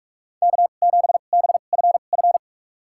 Morse code, heard as a keyed tone, about 46 wpm, spaced as for 33 wpm, 700 Hz. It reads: K7BFF
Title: USA Call Signs - 5 Characters - 2x Character Spacing - 45wpm